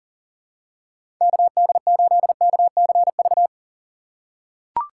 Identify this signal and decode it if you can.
Morse code, heard as a keyed tone, about 38 wpm, 700 Hz. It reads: KD8KCV
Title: USA Call Signs - 2x Character Spacing - 40wpm